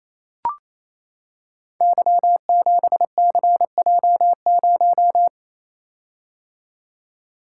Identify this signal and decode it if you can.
Morse code, heard as a keyed tone, about 29 wpm, 700 Hz. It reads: Y7CJ0
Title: Code Groups - 5 Letter:Numbers - 28wpm